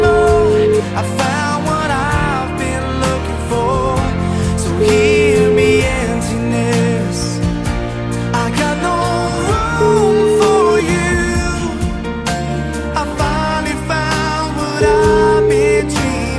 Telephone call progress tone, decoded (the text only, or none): ringback tone